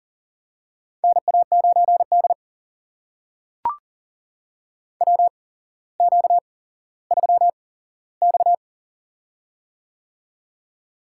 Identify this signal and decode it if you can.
Morse code, heard as a keyed tone, about 40 wpm, spaced as for 17 wpm, 700 Hz. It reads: NA9D WQ3X
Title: USA Call Signs - 8x Character Spacing - 40wpm